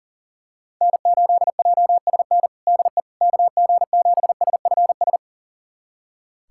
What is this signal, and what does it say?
Morse code, heard as a keyed tone, about 40 wpm, 700 Hz. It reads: N8JSN DE KG7SFS
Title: USA Callsign DE Callsign 40wpm